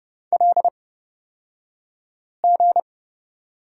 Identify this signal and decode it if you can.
Morse code, heard as a keyed tone, about 30 wpm, spaced as for 5 wpm, 700 Hz. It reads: LG
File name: Lesson 38d - Letter q - Two Characters 30wpm